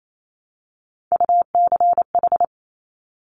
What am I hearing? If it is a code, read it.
Morse code, heard as a keyed tone, about 28 wpm, 700 Hz. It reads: UCH
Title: Top Three Letter Combinations in Words 28wpm